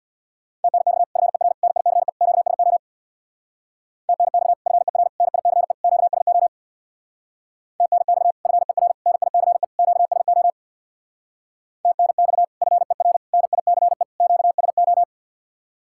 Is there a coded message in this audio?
Yes: Morse code from 0.6 to 15.0 s, heard as a keyed tone, about 66 wpm, 700 Hz. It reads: TNX FER NICE QSO TNX FER NICE QSO TNX FER NICE QSO TNX FER NICE QSO